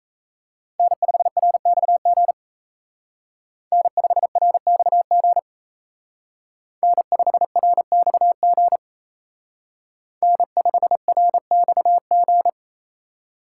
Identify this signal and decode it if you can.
Morse code, heard as a keyed tone, about 32 wpm, 700 Hz. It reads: N5RXG N5RXG N5RXG N5RXG
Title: Speed Racing - USA 1x3 Call Signs 28wpm